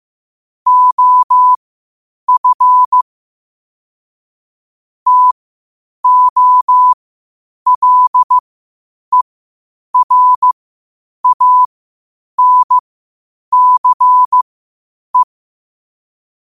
Morse code, heard as a keyed tone, about 15 wpm, spaced as for 8 wpm, 1000 Hz. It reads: OF TOLERANCE
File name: qotdfiles / 8wpm